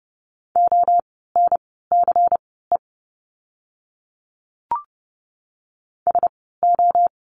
Morse code, heard as a keyed tone, about 30 wpm, spaced as for 17 wpm, 700 Hz. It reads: ONCE SO
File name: course / 30:10wpm / 108 Lesson 18f - Letter m - Words 30wpm